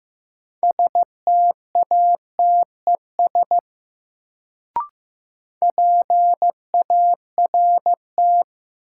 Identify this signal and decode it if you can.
Morse code, heard as a keyed tone, about 15 wpm, 700 Hz. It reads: STATES PART
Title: Sets of 2 Words - New Word Drill - 15wpm